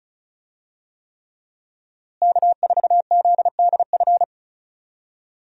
Morse code, heard as a keyed tone, about 36 wpm, 700 Hz. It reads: K4ZDF
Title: USA Call Signs - 5 Characters - Repeated 3x - 35wpm